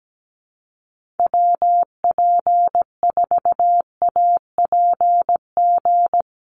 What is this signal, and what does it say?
Morse code, heard as a keyed tone, about 17 wpm, 700 Hz. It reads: WP4APG
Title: USA Call Signs Repeated 3x 17wpm